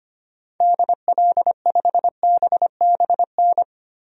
Morse code, heard as a keyed tone, about 25 wpm, 700 Hz. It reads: DL5BBN
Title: DXCC-100 Call Signs - 6 Characters - 25wpm